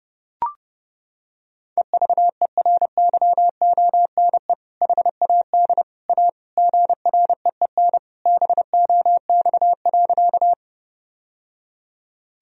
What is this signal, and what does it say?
Morse code, heard as a keyed tone, about 30 wpm, 700 Hz. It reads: EVERYONE HAD A GREEN BOX.
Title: Sentences from Top 1000 Words - Review Part 2 - 30wpm